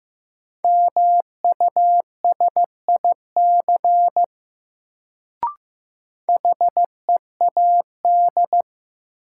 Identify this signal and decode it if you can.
Morse code, heard as a keyed tone, about 15 wpm, 700 Hz. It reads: MUSIC HEAD
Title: Sentences from Top 300 Words - New Word Drill - 15wpm